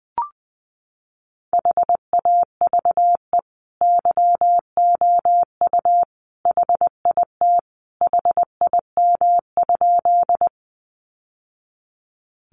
Morse code, heard as a keyed tone, about 20 wpm, 700 Hz. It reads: HAVE YOU HIT HIM?
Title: Sentences from Top 1200 Words 20wpm